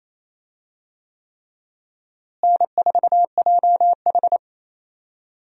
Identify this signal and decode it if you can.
Morse code, heard as a keyed tone, about 28 wpm, 700 Hz. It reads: N4JH
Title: USA Call Signs - 4 Characters - Repeated 3x - 28wpm